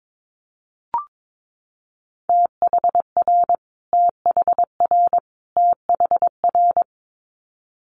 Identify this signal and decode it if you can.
Morse code, heard as a keyed tone, about 22 wpm, 700 Hz. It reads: THR THR THR
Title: Multi-letter Phonemes Repeated 3x 22wpm